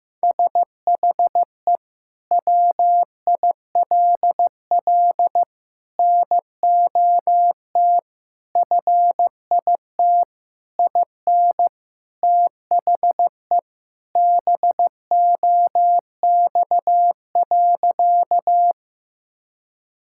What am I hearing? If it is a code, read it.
Morse code, heard as a keyed tone, about 15 wpm, 700 Hz. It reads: SHE WILL NOT FIT IN THE BOX.
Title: Sentences from Top 1000 Words - Review Part 5 - 15wpm